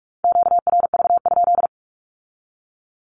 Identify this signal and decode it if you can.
Morse code, heard as a keyed tone, about 46 wpm, 700 Hz. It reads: QRV?
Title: Q-Codes with Spoken Meaning - 45wpm